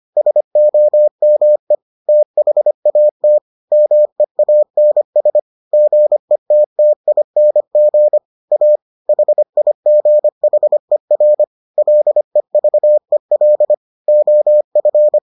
Morse code, heard as a keyed tone, about 25 wpm, 600 Hz. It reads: SOME THAT MEANS GETTING A HIGHER LEVEL OF